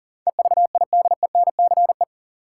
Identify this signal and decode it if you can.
Morse code, heard as a keyed tone, about 40 wpm, 700 Hz. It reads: EVIDENCE
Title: Sets of 5 Words Encore - New Word Drill - 40wpm